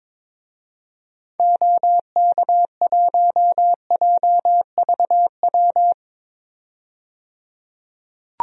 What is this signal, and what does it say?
Morse code, heard as a keyed tone, about 22 wpm, 700 Hz. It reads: OK1JVW
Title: DXCC-100 Call Signs - 6 Characters - 22wpm